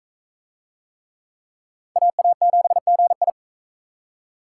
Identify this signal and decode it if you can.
Morse code, heard as a keyed tone, about 41 wpm, 700 Hz. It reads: AA7GI